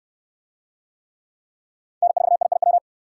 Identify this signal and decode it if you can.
Morse code, heard as a keyed tone, about 69 wpm, 700 Hz. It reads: N4IIW